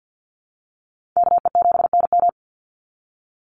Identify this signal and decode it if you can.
Morse code, heard as a keyed tone, about 49 wpm, 700 Hz. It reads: KE7NM
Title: USA Call Signs 50wpm